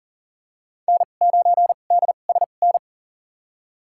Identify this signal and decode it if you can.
Morse code, heard as a keyed tone, about 40 wpm, spaced as for 29 wpm, 700 Hz. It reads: N9DSN
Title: USA Call Signs - 5 Characters - 2x Character Spacing - 40wpm